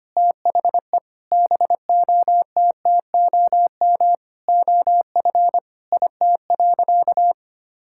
Morse code, heard as a keyed tone, about 25 wpm, 700 Hz. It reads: THE BOTTOM OF IT.
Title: Sentences from Top 1000 Words - Review Part 6 - 25wpm